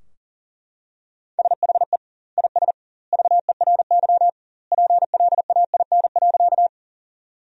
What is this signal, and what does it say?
Morse code, heard as a keyed tone, about 40 wpm, 700 Hz. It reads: SHE IS VERY PLAIN.
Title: 2x Word-Spacing - Sentences from Top 400 Words - 40wpm